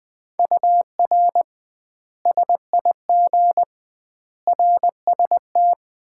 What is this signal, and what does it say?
Morse code, heard as a keyed tone, about 21 wpm, 700 Hz. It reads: UR SIG RST
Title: Common QSO Phrases - 2x Word-Spacing - 20wpm